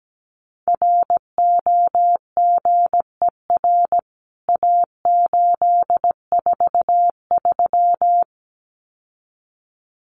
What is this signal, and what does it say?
Morse code, heard as a keyed tone, about 17 wpm, 700 Hz. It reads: ROGER A843
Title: CWT Exchange 17wpm